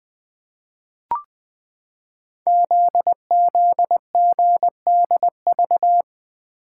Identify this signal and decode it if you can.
Morse code, heard as a keyed tone, about 20 wpm, 700 Hz. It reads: ZZGDV
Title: Code Groups - 5 Letters - 20wpm